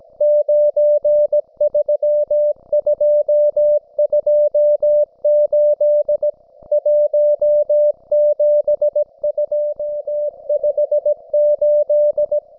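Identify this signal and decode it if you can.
Morse code, heard as a keyed tone, about 17 wpm, 590 Hz. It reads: 93228 17258
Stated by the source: Russian military Morse station, M32, RDL, on 5268 kHz.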